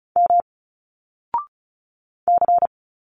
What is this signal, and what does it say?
Morse code, heard as a keyed tone, about 35 wpm, 700 Hz. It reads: M C
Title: Single Letters 35wpm